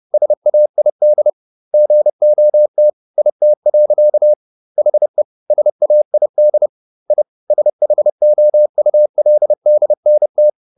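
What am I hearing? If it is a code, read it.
Morse code, heard as a keyed tone, about 30 wpm, 600 Hz. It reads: SAID GOT IT. HE SAID I SHOULDNT
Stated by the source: Morse code at 30 wpm, the novel Never by Ken Follett in English, from CW med Gnister.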